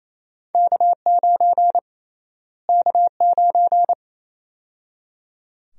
Morse code, heard as a keyed tone, about 28 wpm, 700 Hz. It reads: K9 K9
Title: USA Call Sign Prefixes - Repeated 2x - 28wpm